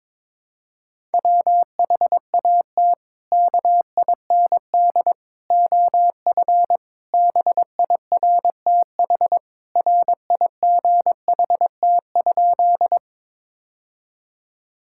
Morse code, heard as a keyed tone, about 22 wpm, 700 Hz. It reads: WHAT KIND OF BIRTH RIGHT?
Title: Sentences from Top 1000 Words - Review Part 5 - 22wpm